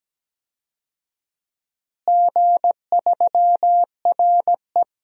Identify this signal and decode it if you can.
Morse code, heard as a keyed tone, about 17 wpm, 700 Hz. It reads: G3RE
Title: DXCC-100 Call Signs - 4 Characters - Repeated 3x - 17wpm